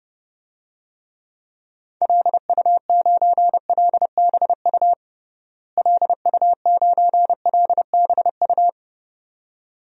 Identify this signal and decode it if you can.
Morse code, heard as a keyed tone, about 30 wpm, 700 Hz. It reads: LU9LBU LU9LBU
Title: DXCC-100 Call Signs - 6 Characters - Repeated 2x - 30wpm